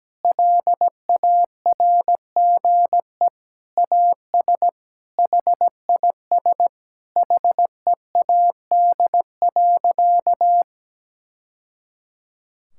Morse code, heard as a keyed tone, about 17 wpm, 700 Hz. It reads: LARGE AS HIS HEAD.
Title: Sentences from Top 1000 Words - Review Part 6 - 17wpm